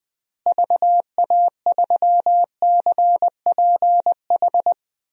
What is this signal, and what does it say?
Morse code, heard as a keyed tone, about 21 wpm, 700 Hz. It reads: VA3CPH
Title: DXCC-100 Call Signs - 6 Characters - Repeated 3x - 20wpm